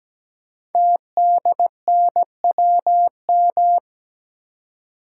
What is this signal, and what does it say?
Morse code, heard as a keyed tone, about 17 wpm, 700 Hz. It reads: TDNWM
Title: Code Groups - 5 Letters - 17wpm